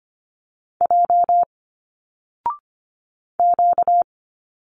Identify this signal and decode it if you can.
Morse code, heard as a keyed tone, about 25 wpm, 700 Hz. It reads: J Q
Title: Single Letters 25wpm